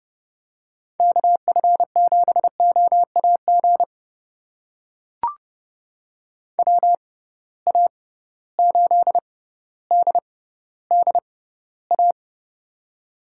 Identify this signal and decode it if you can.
Morse code, heard as a keyed tone, about 30 wpm, spaced as for 15 wpm, 700 Hz. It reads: KF7OAG WA8DDA
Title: USA Call Signs - 6 Characters - 6x Character Spacing - 30wpm